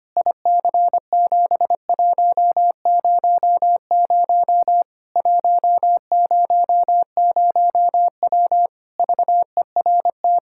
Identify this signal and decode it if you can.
Morse code, heard as a keyed tone, about 25 wpm, 700 Hz. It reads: IC7100 100W VERT
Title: K1USN SST QSO Practice 25wpm